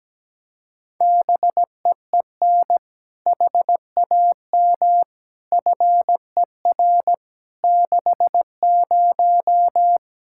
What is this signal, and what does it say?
Morse code, heard as a keyed tone, about 17 wpm, 700 Hz. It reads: BEEN HAM FER 60
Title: K1USN SST QSO Practice 17wpm